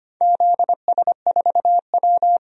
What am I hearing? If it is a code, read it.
Morse code, heard as a keyed tone, about 25 wpm, 700 Hz. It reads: ZS4W
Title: DXCC-100 Call Signs - 4 Characters - 25wpm